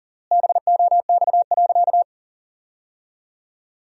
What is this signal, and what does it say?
Morse code, heard as a keyed tone, about 40 wpm, 700 Hz. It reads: BOX.
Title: 3x Word-Spacing - Sentences from Top 500 Words - 40wpm